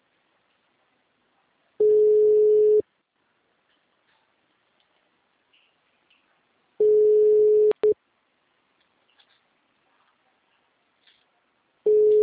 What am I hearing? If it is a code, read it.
Ringback tone (Europe, 425 Hz).